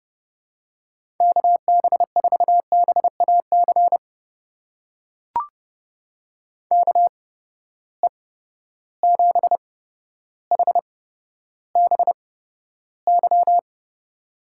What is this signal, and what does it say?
Morse code, heard as a keyed tone, about 30 wpm, spaced as for 13 wpm, 700 Hz. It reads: KB4BAC KE7HBY
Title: USA Call Signs - 6 Characters - 8x Character Spacing - 30wpm